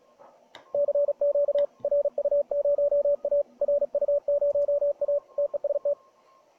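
Morse code, heard as a keyed tone, about 38 wpm, 590 Hz, with radio noise. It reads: CQ RU0A RU0A TEST
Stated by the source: CQ WW DX CW 2013 contest QSO, transceiver at 80 mW QRP; antennas vertical 14 MHz, dipole-sloper 14 MHz, LW 41 m.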